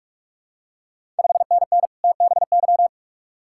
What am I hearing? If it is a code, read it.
Morse code, heard as a keyed tone, about 46 wpm, 700 Hz. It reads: 5NN TBY
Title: Ontario QSO Party Exchange 45wpm